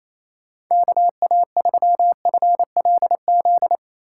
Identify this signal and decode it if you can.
Morse code, heard as a keyed tone, about 28 wpm, 700 Hz. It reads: KA3FLZ
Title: USA Call Signs - 6 Characters - 6x Character Spacing - 28wpm